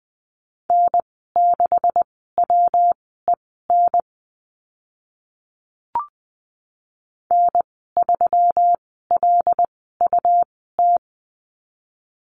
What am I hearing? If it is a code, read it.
Morse code, heard as a keyed tone, about 19 wpm, spaced as for 14 wpm, 700 Hz. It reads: N6WEN N3LUT
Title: 167 Lesson 27g - Number 7 - Call Signs 20wpm